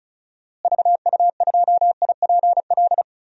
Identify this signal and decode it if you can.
Morse code, heard as a keyed tone, about 36 wpm, 700 Hz. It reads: VU2IPL
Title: DXCC-100 Call Signs - 6 Characters - Repeated 3x - 35wpm